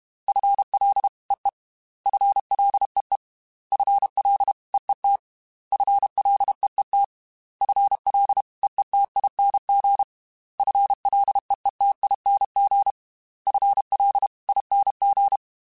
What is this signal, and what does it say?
Morse code, heard as a keyed tone, about 32 wpm, 800 Hz. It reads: FLEE FLEE FLEET FLEET FLEETING FLEETING FLING
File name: FL / FL mp3s / FL Prefixes 3x2 sp @30